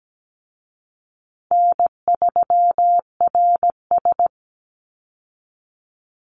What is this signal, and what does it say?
Morse code, heard as a keyed tone, about 17 wpm, 700 Hz. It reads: N3RS
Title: Easy Call Signs Repeated 2x - 17wpm